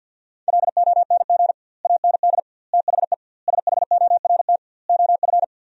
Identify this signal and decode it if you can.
Morse code, heard as a keyed tone, about 50 wpm, 700 Hz. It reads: LONG AND THE SHORT OF
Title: 2x Word-Spacing - Sentences from Top 400 Words - 50wpm